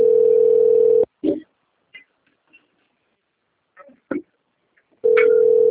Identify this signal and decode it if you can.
Ringback tone (North America), 440 + 480 Hz, with speech.
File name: rg-1600-unknown